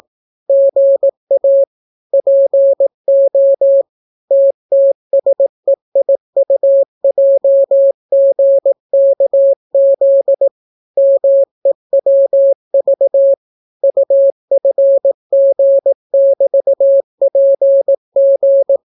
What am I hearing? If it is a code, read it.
Morse code, heard as a keyed tone, about 18 wpm, 550 Hz. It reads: GA PO TTSEIUJGKZ MEWV UFG<BT>PG